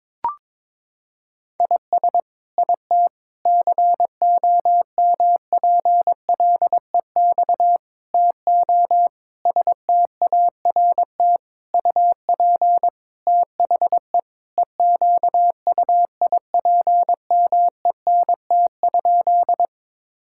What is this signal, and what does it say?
Morse code, heard as a keyed tone, about 22 wpm, 700 Hz. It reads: IS IT COMPLEX TO START UP THE EQUIPMENT?
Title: Sentences from Top 1000 Words - Review Part 6 - 22wpm